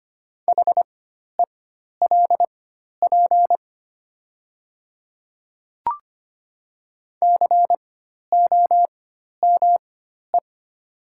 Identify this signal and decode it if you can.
Morse code, heard as a keyed tone, about 25 wpm, spaced as for 12 wpm, 700 Hz. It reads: HELP COME
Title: 4x Character Spacing - Top 200 Words - 25wpm